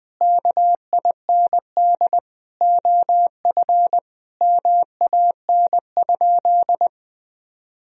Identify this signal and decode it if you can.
Morse code, heard as a keyed tone, about 20 wpm, 700 Hz. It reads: KIND OF MAN?
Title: Sentences from Top 1000 Words - Review Part 4 - 20wpm